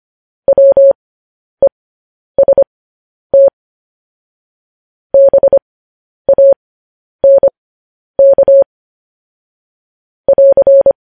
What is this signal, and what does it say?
Morse code, heard as a keyed tone, about 25 wpm, spaced as for 10 wpm, 550 Hz. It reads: WEST BANK <AR>
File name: podcast / News headlines-1x-25x10